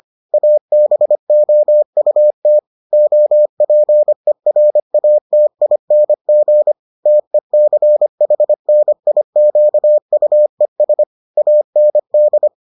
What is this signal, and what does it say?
Morse code, heard as a keyed tone, about 25 wpm, 600 Hz. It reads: ABOUT OPERATING TECHNIQUES AND